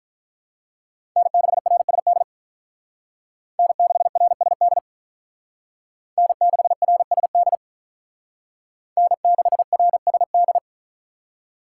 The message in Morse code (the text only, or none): N6RSD N6RSD N6RSD N6RSD